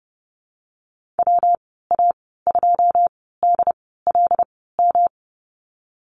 Morse code, heard as a keyed tone, about 30 wpm, spaced as for 17 wpm, 700 Hz. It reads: WA2DLM
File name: Lesson 19g - Letter w - Call Signs 30wpm